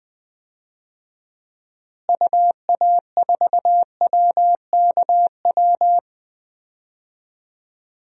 Morse code, heard as a keyed tone, about 20 wpm, 700 Hz. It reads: UA4WKW
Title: DXCC-100 Call Signs - 6 Characters - Repeated 3x - 20wpm